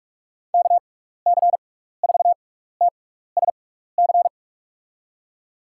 Morse code, heard as a keyed tone, about 42 wpm, spaced as for 15 wpm, 700 Hz. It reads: KC4TSC
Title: USA Call Signs - 6 Characters - 6x Character Spacing - 45wpm